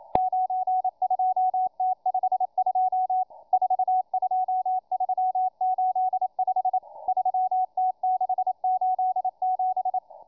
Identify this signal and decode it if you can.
Morse code, heard as a keyed tone, about 27 wpm, 740 Hz, with radio noise. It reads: 92T52 42385 3T687